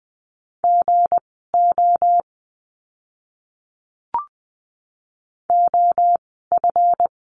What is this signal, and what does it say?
Morse code, heard as a keyed tone, about 20 wpm, spaced as for 14 wpm, 700 Hz. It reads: GO OF
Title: 199 Lesson 32f - Letter k - Words 20wpm